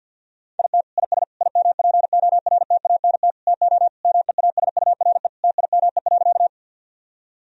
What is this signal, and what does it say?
Morse code, heard as a keyed tone, about 49 wpm, 700 Hz. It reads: IT IS IMPORTANT TO MEASURE TIME.
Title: Sentences from Top 1000 Words - Review Part 2 - 50wpm